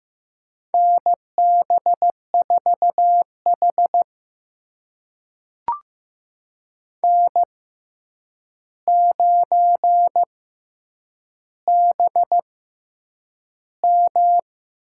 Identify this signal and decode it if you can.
Morse code, heard as a keyed tone, about 15 wpm, spaced as for 8 wpm, 700 Hz. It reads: NB4H N9BM
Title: USA Call Signs - 4 Characters - 6x Character Spacing - 15wpm